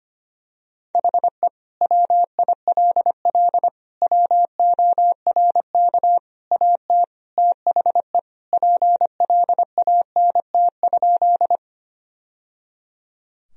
Morse code, heard as a keyed tone, about 25 wpm, 700 Hz. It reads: HE WILL WORK AT THE PLANT?
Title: Sentences from Top 1000 Words - Review Part 1 - 25wpm